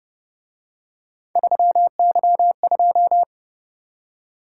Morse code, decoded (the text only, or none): 3Y2